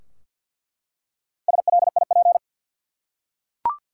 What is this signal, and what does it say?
Morse code, heard as a keyed tone, about 48 wpm, 700 Hz. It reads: SLIP
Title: Four Letter Words - 50wpm